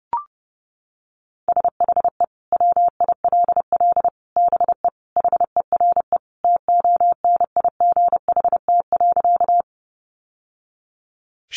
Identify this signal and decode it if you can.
Morse code, heard as a keyed tone, about 30 wpm, 700 Hz. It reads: SHE WILL BE HERE TONIGHT.
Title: Sentences from Top 600 Words 30wpm